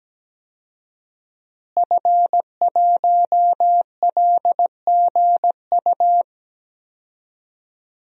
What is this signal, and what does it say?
Morse code, heard as a keyed tone, about 17 wpm, 700 Hz. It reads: F1LGU
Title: DXCC-100 Call Signs - 5 Characters - Repeated 2x - 17wpm